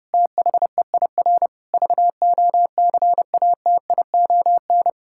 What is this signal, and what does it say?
Morse code, heard as a keyed tone, about 30 wpm, 700 Hz. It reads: THEIR VOCATION
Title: Pangram Challenge v2 - 30wpm